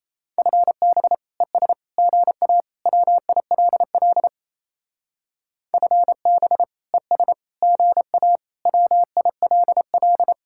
Morse code, heard as a keyed tone, about 37 wpm, 700 Hz. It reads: FB ES GA WILL FB ES GA WILL